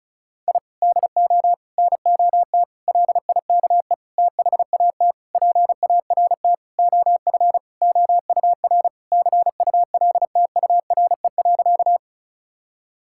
Morse code, heard as a keyed tone, about 36 wpm, 700 Hz. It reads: I DO NOT LIKE THAT PART OF OUR CULTURE.
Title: Sentences from Top 1000 Words - Review Part 6 - 35wpm